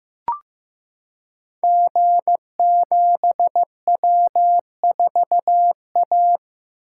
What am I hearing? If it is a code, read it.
Morse code, heard as a keyed tone, about 15 wpm, 700 Hz. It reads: G7W4A
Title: Code Groups - 5 Letter:Numbers - 15wpm